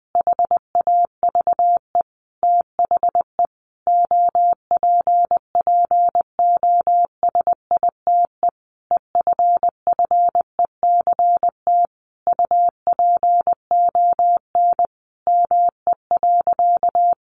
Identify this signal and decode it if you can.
Morse code, heard as a keyed tone, about 20 wpm, 700 Hz. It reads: HAVE THE OPPOSITE EFFECT UPON ME.
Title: Sentences from Top 1300 Words 20wpm